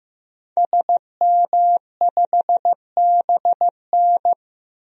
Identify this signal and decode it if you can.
Morse code, heard as a keyed tone, about 15 wpm, 700 Hz. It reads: SM5BN
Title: DXCC-100 Call Signs - 5 Characters - 15wpm